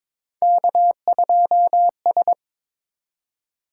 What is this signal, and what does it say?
Morse code, heard as a keyed tone, about 22 wpm, 700 Hz. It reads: K2S
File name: practice / 22wpm / Three Letter-Number Combinations - 22wpm